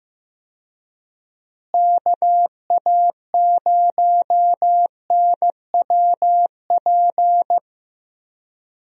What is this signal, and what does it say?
Morse code, heard as a keyed tone, about 16 wpm, 700 Hz. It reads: KA0NWP
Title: USA Call Signs - 6 Characters - Repeated 2x - 15wpm